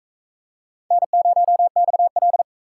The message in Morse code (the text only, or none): N0XL